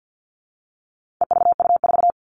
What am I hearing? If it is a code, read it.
Morse code, heard as a keyed tone, about 50 wpm, 700 Hz. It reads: EVU4
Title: All Cut Numbers - 4-Digit 50wpm